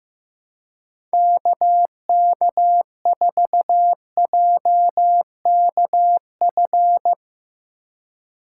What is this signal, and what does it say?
Morse code, heard as a keyed tone, about 15 wpm, 700 Hz. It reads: KK4JKF
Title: USA Call Signs - 3x Character Spacing - 15wpm